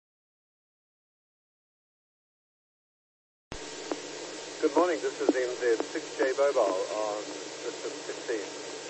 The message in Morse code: none

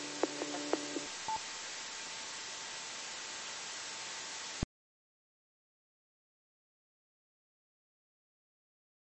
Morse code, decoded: none

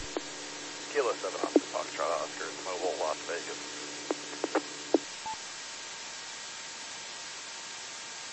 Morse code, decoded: none

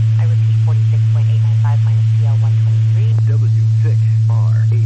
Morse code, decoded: none

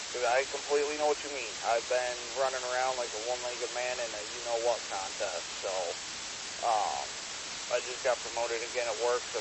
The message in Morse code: none

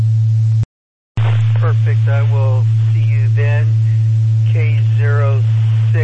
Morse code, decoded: none

